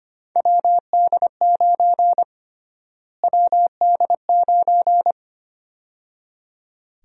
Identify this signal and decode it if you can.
Morse code, heard as a keyed tone, about 25 wpm, 700 Hz. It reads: WD9 WD9
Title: USA Call Sign Prefixes - Repeated 2x - 25wpm